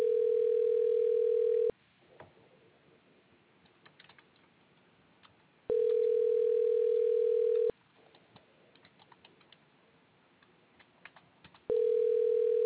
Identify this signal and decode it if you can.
Ringback tone (North America), 440 + 480 Hz.